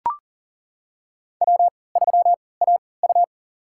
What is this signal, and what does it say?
Morse code, heard as a keyed tone, about 40 wpm, spaced as for 23 wpm, 700 Hz. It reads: W3AU
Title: USA Call Signs - 3x Character Spacing - 40wpm